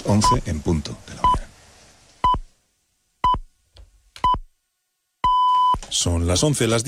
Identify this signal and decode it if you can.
Radio time signal (time pips), with speech. It is heard from 0.2 to 5.7 s.